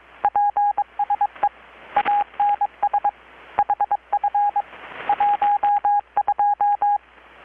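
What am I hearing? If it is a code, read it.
Morse code, heard as a keyed tone, about 22 wpm, 820 Hz, with radio noise. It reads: PSE ANS HF 12
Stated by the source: Morse from KPH's Night of Nights XX maritime radiotelegraphy event, received in Chiapas, Mexico about 3,800 km away on a Kenwood R-600 with a grounded 10 m long-wire antenna.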